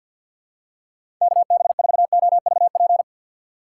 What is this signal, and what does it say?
Morse code, heard as a keyed tone, about 53 wpm, 700 Hz. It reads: KB4OVP
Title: USA Call Signs - 6 Characters - 6x Character Spacing - 50wpm